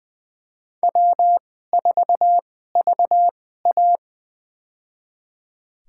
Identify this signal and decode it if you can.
Morse code, heard as a keyed tone, about 20 wpm, spaced as for 14 wpm, 700 Hz. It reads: W4VA
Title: USA Call Signs - 4 Characters - 2x Character Spacing - 20wpm